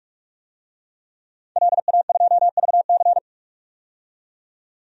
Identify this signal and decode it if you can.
Morse code, heard as a keyed tone, about 46 wpm, 690 Hz. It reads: LA2VC
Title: DXCC-100 Call Signs - 5 Characters - 45wpm